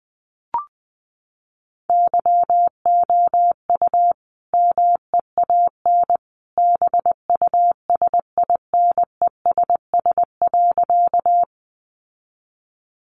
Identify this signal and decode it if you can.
Morse code, heard as a keyed tone, about 19 wpm, 700 Hz. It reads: YOU MEAN BUSINESS.
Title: Sentences from Top 600 Words 20wpm